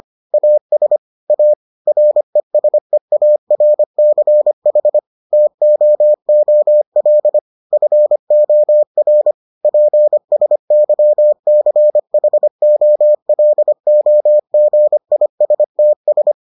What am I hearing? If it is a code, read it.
Morse code, heard as a keyed tone, about 25 wpm, 600 Hz. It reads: AS A RESEARCH TOOL FOR PSYCHOLOGISTS